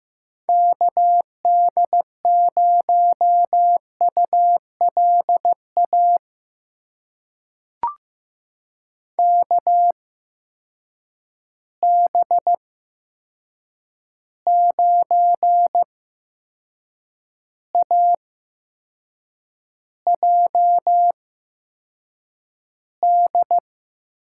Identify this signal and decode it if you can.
Morse code, heard as a keyed tone, about 15 wpm, spaced as for 6 wpm, 700 Hz. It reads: KD0ULA KB9AJD
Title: USA Call Signs - 6 Characters - 8x Character Spacing - 15wpm